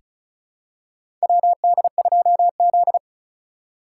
Morse code, heard as a keyed tone, about 36 wpm, 700 Hz. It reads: WD2Z
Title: USA Call Signs - 4 Characters - 2x Character Spacing - 35wpm